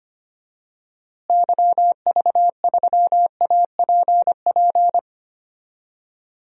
Morse code, heard as a keyed tone, about 25 wpm, 700 Hz. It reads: YV3APP
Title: DXCC-100 Call Signs - 6 Characters - Repeated 2x - 25wpm